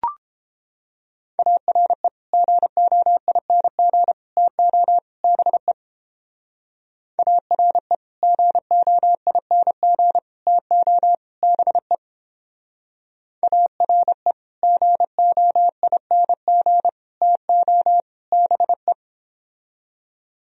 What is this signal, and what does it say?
Morse code, heard as a keyed tone, about 29 wpm, 700 Hz. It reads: ARE GOING TO BE ARE GOING TO BE ARE GOING TO BE